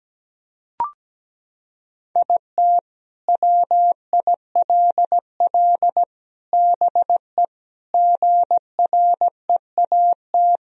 Morse code, heard as a keyed tone, about 17 wpm, 700 Hz. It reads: IT WILL BE GREAT
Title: Sentences from Top 1000 Words - Review Part 4 - 17wpm